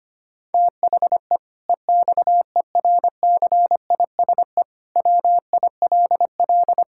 Morse code, heard as a keyed tone, about 25 wpm, 700 Hz. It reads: THE EXERCISE WILL
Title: Sentences from Top 1000 Words - Review Part 6 - 25wpm